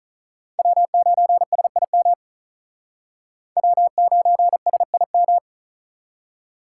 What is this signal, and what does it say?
Morse code, heard as a keyed tone, about 46 wpm, 700 Hz. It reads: W9SIM W9SIM